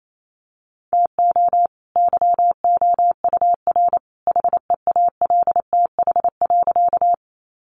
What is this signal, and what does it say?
Morse code, heard as a keyed tone, about 28 wpm, 700 Hz. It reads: TO YOUR HEALTH.
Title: Sentences from Top 600 Words 28wpm